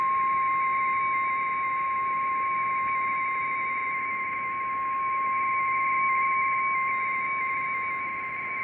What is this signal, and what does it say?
Two-tone teleprinter signal, about 170 Hz shift.
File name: BiTx / mp3bitx / 1140 UA9NHK rtty2